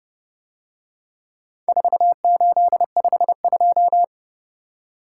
Morse code, heard as a keyed tone, about 30 wpm, 700 Hz. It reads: 4852